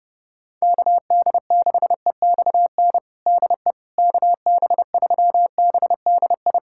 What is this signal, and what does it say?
Morse code, heard as a keyed tone, about 32 wpm, 700 Hz. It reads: KD6EXN DE KB3BDI
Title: USA Callsign DE Callsign 30wpm